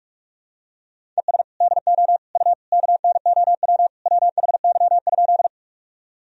Morse code, heard as a keyed tone, about 41 wpm, 700 Hz. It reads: ES DO U KNOW WHY?
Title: Sentences from Top 200 Words and Abbreviations 45wpm